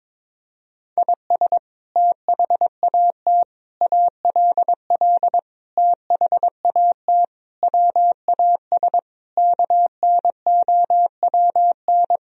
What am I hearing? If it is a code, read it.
Morse code, heard as a keyed tone, about 22 wpm, 700 Hz. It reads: IS THAT ALL THAT WAS KNOWN
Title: Sentences from Top 1000 Words - Review Part 3 - 22wpm